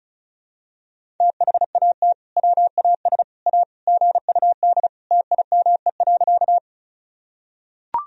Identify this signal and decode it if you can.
Morse code, heard as a keyed tone, about 34 wpm, 700 Hz. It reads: THAT WAS A GUD TIME.
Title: Sentences from Top 200 Words and Abbreviations 35wpm